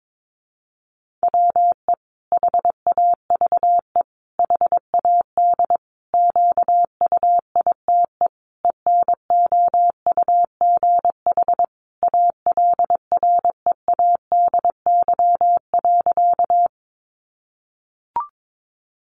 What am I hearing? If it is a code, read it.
Morse code, heard as a keyed tone, about 22 wpm, 700 Hz. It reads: WE HAVE HAD QUITE ENOUGH ALREADY.
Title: Sentences from Top 700 Words 22wpm